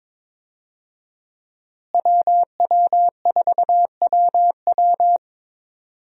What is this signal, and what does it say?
Morse code, heard as a keyed tone, about 22 wpm, 700 Hz. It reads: WW4WW
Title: USA Call Signs - 5 Characters - Repeated 2x - 22wpm